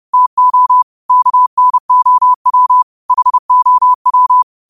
Morse code, heard as a keyed tone, about 30 wpm, 1000 Hz. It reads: TO KNOW HOW
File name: qotdfiles / 30wpm